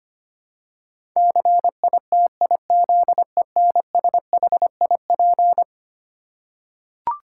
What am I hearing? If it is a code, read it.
Morse code, heard as a keyed tone, about 25 wpm, 700 Hz. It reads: CITIZENSHIP
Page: morse-code-ninja-practice-25wpm